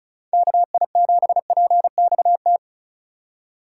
Morse code, heard as a keyed tone, about 34 wpm, 700 Hz. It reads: KI7PXT
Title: USA Call Signs - 6 Characters - 2x Character Spacing - 35wpm